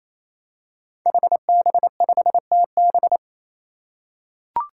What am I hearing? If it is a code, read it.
Morse code, heard as a keyed tone, about 27 wpm, 700 Hz. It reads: HB5TB
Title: Five Challenging Letter-Number Combinations 28wpm